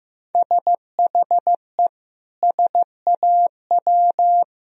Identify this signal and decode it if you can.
Morse code, heard as a keyed tone, about 15 wpm, 700 Hz. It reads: SHE SAW
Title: Sentences from Top 1000 Words - Review Part 4 - 15wpm